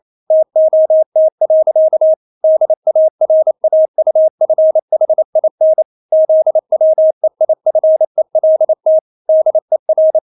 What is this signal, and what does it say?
Morse code, heard as a keyed tone, about 28 wpm, 630 Hz. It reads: TOT. DARAUFHIN ZWEIFELT DER
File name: Witz9 28wpm